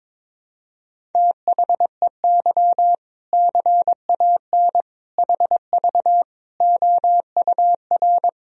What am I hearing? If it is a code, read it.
Morse code, heard as a keyed tone, about 22 wpm, 700 Hz. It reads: THEY CAN HV OUR
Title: Sentences from Top 400 Words and Abbreviations 22wpm